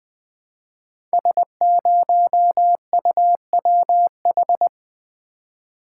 Morse code, heard as a keyed tone, about 20 wpm, 700 Hz. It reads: S0UWH
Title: Code Groups - 5 Letter:Numbers - 20wpm